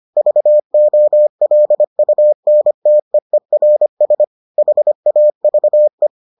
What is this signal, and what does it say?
Morse code, heard as a keyed tone, about 25 wpm, 600 Hz. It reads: VOLUNTEERS HAVE